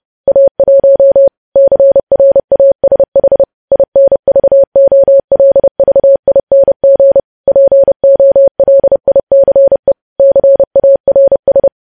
Morse code, heard as a keyed tone, about 30 wpm, 550 Hz. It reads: A1 CRASH INVOLVING POLICE CARS
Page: news headlines in morse code at 30 wpm